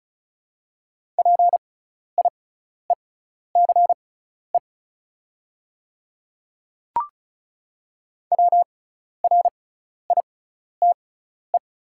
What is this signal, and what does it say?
Morse code, heard as a keyed tone, about 36 wpm, spaced as for 12 wpm, 690 Hz. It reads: PIECE WRITE
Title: 6x Character Spacing - Top 500 Words - 35wpm